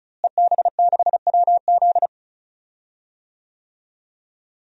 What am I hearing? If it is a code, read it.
Morse code, heard as a keyed tone, about 35 wpm, 700 Hz. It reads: EB6WZ
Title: DXCC-100 Call Signs - 5 Characters - Repeated 3x - 35wpm